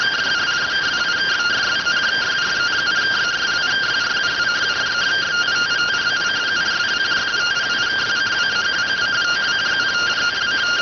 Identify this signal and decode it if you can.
Two-tone teleprinter signal, about 170 Hz shift.